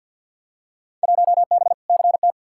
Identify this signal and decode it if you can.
Morse code, heard as a keyed tone, about 53 wpm, 700 Hz. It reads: 1B CT
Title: Field Day Exchanges Encore - 50wpm